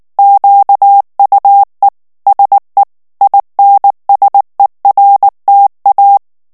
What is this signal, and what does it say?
Morse code, heard as a keyed tone, about 19 wpm, 800 Hz. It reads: QUE SE INSERTA
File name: morse mp3 / ESPAGNOL TEXTE EN CLAIR 2 VITESSE 18 MOTS MINUTES